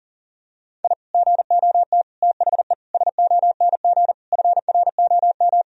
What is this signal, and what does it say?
Morse code, heard as a keyed tone, about 40 wpm, 700 Hz. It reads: I GOT THE SONG FROM
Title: Sentences from Top 1000 Words - Review Part 2 - 40wpm